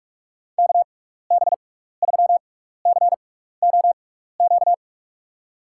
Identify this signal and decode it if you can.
Morse code, heard as a keyed tone, about 47 wpm, spaced as for 16 wpm, 700 Hz. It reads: KB3COQ
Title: USA Call Signs - 6 Characters - 6x Character Spacing - 45wpm